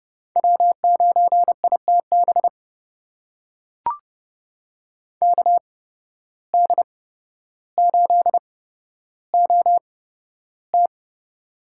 Morse code, heard as a keyed tone, about 30 wpm, spaced as for 13 wpm, 700 Hz. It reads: W9ITB KD8OT
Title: USA Call Signs - 5 Characters - 8x Character Spacing - 30wpm